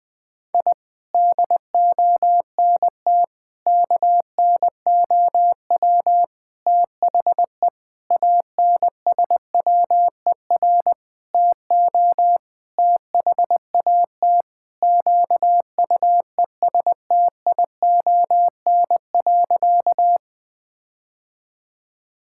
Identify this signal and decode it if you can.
Morse code, heard as a keyed tone, about 21 wpm, 700 Hz. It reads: I DONT KNOW THE ANSWER TO THAT QUESTION.
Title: Sentences with 7-Word Phrases - 20wpm